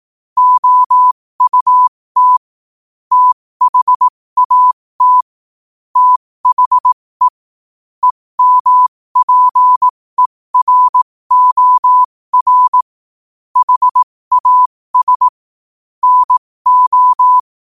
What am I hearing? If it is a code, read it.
Morse code, heard as a keyed tone, about 18 wpm, spaced as for 15 wpm, 1000 Hz. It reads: OUT THAT THE EMPEROR HAS NO
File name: 15wpm